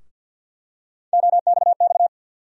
Morse code, heard as a keyed tone, about 52 wpm, 700 Hz. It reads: OXX
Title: USA Call Sign Postfixes - 3 Letters 50wpm